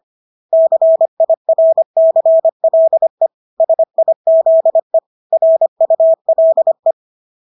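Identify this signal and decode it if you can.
Morse code, heard as a keyed tone, about 25 wpm, 650 Hz. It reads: CIRCLE SIZE RULE
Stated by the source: Morse practice of English words at 25 wpm from CW med Gnister.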